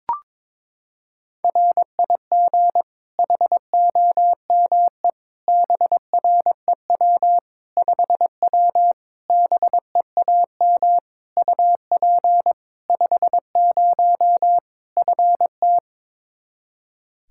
Morse code, heard as a keyed tone, about 22 wpm, 700 Hz. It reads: RIG HOME BREW 5W BEAM UP 50 FT
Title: Practice QSO - Exchange 2 - Info - 22wpm